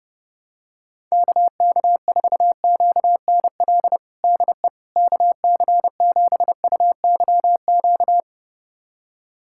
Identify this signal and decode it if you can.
Morse code, heard as a keyed tone, about 30 wpm, 700 Hz. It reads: KK4QNL DE KC7UYQ
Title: USA Callsign DE Callsign - 6 Characters - 30wpm